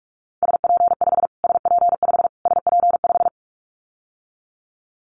Morse code, heard as a keyed tone, about 44 wpm, 700 Hz. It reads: SP5 SP5 SP5
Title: Callsign Prefixes for Top 25 DXCC - Repeated 3x - 45wpm